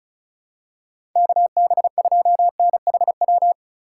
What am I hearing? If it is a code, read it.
Morse code, heard as a keyed tone, about 34 wpm, 700 Hz. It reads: KB2NHW
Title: USA Call Signs - 6 Characters - 4x Character Spacing - 35wpm